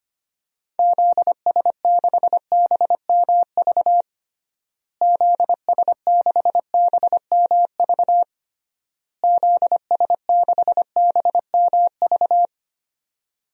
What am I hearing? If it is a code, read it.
Morse code, heard as a keyed tone, about 25 wpm, 700 Hz. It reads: ZS6BMV ZS6BMV ZS6BMV
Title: DXCC-100 Call Signs - 6 Characters - Repeated 3x - 25wpm